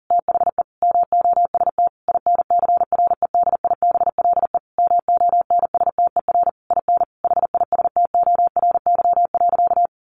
Morse code, heard as a keyed tone, about 40 wpm, 700 Hz. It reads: THE MOST INCREDIBLE MONSTER IN HISTORY.
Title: A Sound Of Thunder 40wpm